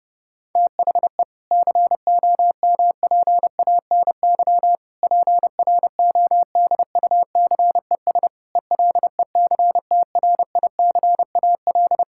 Morse code, heard as a keyed tone, about 30 wpm, 700 Hz. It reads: THE COMPANY PRODUCES ELECTRICAL
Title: Sentences from Top 1000 Words - Review Part 6 - 30wpm